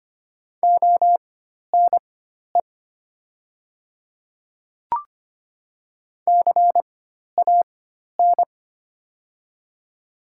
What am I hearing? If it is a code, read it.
Morse code, heard as a keyed tone, about 25 wpm, spaced as for 12 wpm, 700 Hz. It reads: ONE CAN
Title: 4x Character Spacing - Top 100 Words - 25wpm